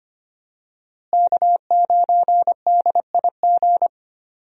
Morse code, heard as a keyed tone, about 25 wpm, 700 Hz. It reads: K9DIG
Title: USA Call Signs - 5 Characters - Repeated 3x - 25wpm